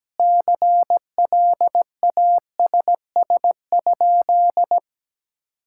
Morse code, heard as a keyed tone, about 17 wpm, 700 Hz. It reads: CLASS?
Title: Sentences from Top 1000 Words - Review Part 5 - 17wpm